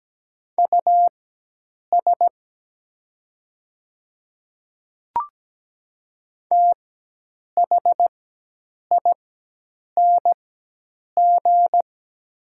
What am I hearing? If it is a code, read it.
Morse code, heard as a keyed tone, about 17 wpm, spaced as for 8 wpm, 700 Hz. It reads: US THING
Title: 4x Character Spacing - Top 500 Words - 17wpm